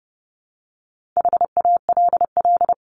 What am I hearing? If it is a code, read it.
Morse code, heard as a keyed tone, about 30 wpm, 700 Hz. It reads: HALL
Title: Top 1751-2000 Words 30wpm